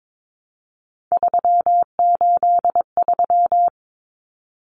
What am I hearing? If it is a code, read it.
Morse code, heard as a keyed tone, about 22 wpm, 700 Hz. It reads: 383
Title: Numbers - 3-Digit 22wpm